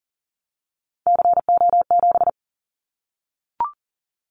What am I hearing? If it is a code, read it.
Morse code, heard as a keyed tone, about 40 wpm, 700 Hz. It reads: CO7
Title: Callsign Prefixes for Top 100 DXCC - 40wpm